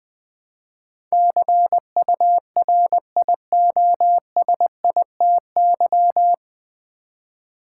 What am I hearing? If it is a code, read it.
Morse code, heard as a keyed tone, about 20 wpm, 700 Hz. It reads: CURIOSITY
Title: Build A Word - 3 Letters At A Time - 9-Letter Words - 20wpm